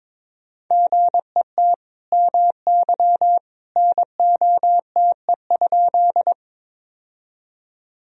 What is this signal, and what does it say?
Morse code, heard as a keyed tone, about 22 wpm, 700 Hz. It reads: GET MY NOTE?
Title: Sentences from Top 1000 Words - Review Part 2 - 22wpm